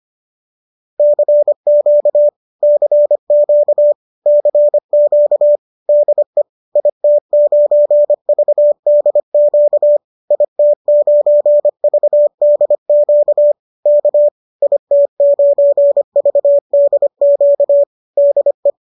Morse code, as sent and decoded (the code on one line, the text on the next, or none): -.-. --.- / -.-. --.- / -.-. --.- / -.. / .. - ----. ...- -.. --.- / .. - ----. ...- -.. --.- / -.- / .. - ----. ...- -.. --.- / -.. .
CQ CQ CQ DE IT9VDQ IT9VDQ K IT9VDQ DE